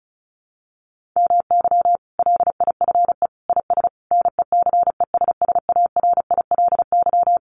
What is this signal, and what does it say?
Morse code, heard as a keyed tone, about 34 wpm, 700 Hz. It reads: MY LIFE IS NECESSARILY
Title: Sentences from Top 1400 Words 35wpm